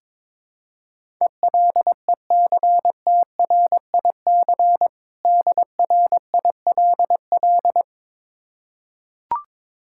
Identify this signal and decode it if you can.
Morse code, heard as a keyed tone, about 21 wpm, 700 Hz. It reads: ELECTRIC DRILL
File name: Difficult Words with Hints - 22wpm